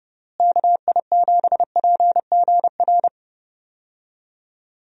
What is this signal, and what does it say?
Morse code, heard as a keyed tone, about 30 wpm, 700 Hz. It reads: KI7PGR
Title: USA Call Signs - 6 Characters - Repeated 3x - 30wpm